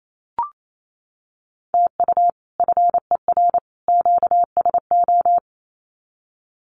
Morse code, heard as a keyed tone, about 28 wpm, 700 Hz. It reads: TU FER QSO
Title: Common QSO Phrases 28wpm